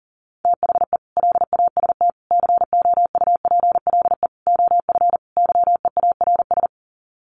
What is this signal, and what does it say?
Morse code, heard as a keyed tone, about 40 wpm, 700 Hz. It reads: THE LAST COUPLE OF YEARS
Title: Sets of 5 Words 40wpm